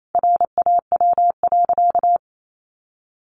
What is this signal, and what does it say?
Morse code, heard as a keyed tone, about 28 wpm, 700 Hz. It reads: RAW.
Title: Sentences from Top 1500 Words 28wpm